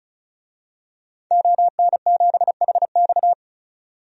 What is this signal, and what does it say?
Morse code, heard as a keyed tone, about 36 wpm, 700 Hz. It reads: ON7HX
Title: DXCC-100 Call Signs - 5 Characters - 35wpm